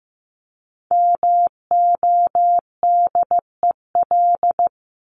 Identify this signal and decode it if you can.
Morse code, heard as a keyed tone, about 15 wpm, 700 Hz. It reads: MODEL
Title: Top 501-750 Words 15wpm